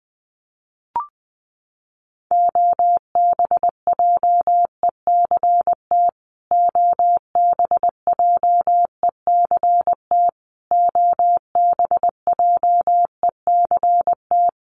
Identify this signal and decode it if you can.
Morse code, heard as a keyed tone, about 20 wpm, 700 Hz. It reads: OBJECT OBJECT OBJECT
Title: Top 500 Words Repeated 20wpm